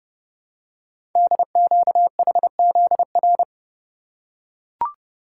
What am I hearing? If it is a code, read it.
Morse code, heard as a keyed tone, about 28 wpm, 700 Hz. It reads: DQHZR
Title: Code Groups - 5 Letter:Numbers - 30wpm